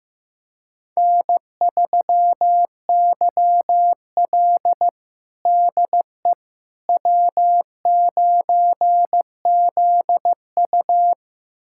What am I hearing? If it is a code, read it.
Morse code, heard as a keyed tone, about 16 wpm, 700 Hz. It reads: N3YL DE W9ZU
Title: USA Callsign DE Callsign - 4 Characters - 15wpm